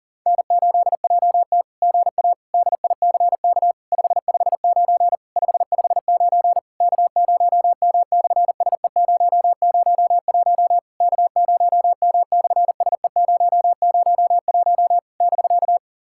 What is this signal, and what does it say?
Morse code, heard as a keyed tone, about 44 wpm, 700 Hz. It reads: N8JT GA DICK 559 559 K0M/SE001 K0M/SE001 <BK>
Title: SOTA QSO Practice 40wpm